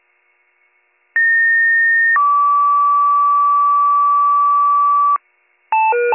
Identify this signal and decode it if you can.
Two-tone page, heard as a one-second tone, then a three-second tone, about 1850 then 1150 Hz, with noise. It is heard from 1.2 to 5.2 s.